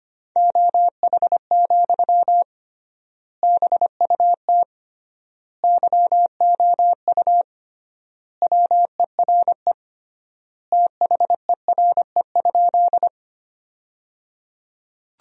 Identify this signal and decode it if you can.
Morse code, heard as a keyed tone, about 25 wpm, 700 Hz. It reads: OH, BUT YOU WERE THERE?
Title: 3x Word-Spacing - Sentences from Top 500 Words - 25wpm